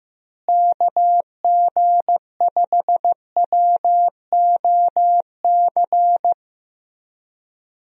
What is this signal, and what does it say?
Morse code, heard as a keyed tone, about 15 wpm, 700 Hz. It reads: KG5WOC
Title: USA Call Signs - 6 Characters - Repeated 3x - 15wpm